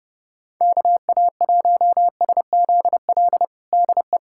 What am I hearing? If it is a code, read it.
Morse code, heard as a keyed tone, about 32 wpm, 700 Hz. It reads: KA1SZL DE
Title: USA Callsign DE Callsign - 6 Characters - 30wpm